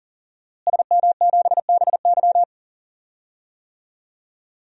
Morse code, heard as a keyed tone, about 40 wpm, 700 Hz. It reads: SM7BY
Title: DXCC-100 Call Signs - 5 Characters - Repeated 2x - 40wpm